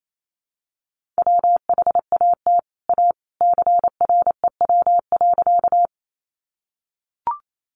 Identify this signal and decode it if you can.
Morse code, heard as a keyed tone, about 28 wpm, 700 Hz. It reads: WHAT A CREW.
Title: Sentences from Top 1500 Words 28wpm